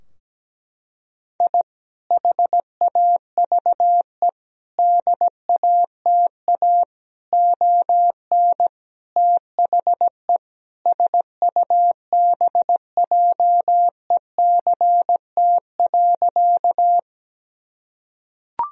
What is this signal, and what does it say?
Morse code, heard as a keyed tone, about 17 wpm, 700 Hz. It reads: I HAVE DATA ON THE SUBJECT.
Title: Sentences from Top 1000 Words - Review Part 3 - 17wpm